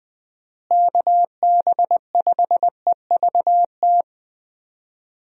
Morse code, heard as a keyed tone, about 21 wpm, 700 Hz. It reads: KB5EVT
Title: USA Call Signs - 6 Characters - Repeated 3x - 20wpm